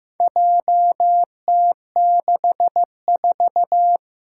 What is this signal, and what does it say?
Morse code, heard as a keyed tone, about 15 wpm, 700 Hz. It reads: JT64
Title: Four Challenging Letter-Number Combinations 15wpm